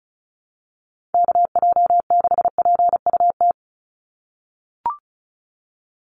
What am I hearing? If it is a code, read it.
Morse code, heard as a keyed tone, about 34 wpm, 700 Hz. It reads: KJ6PUT
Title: USA Call Signs 35wpm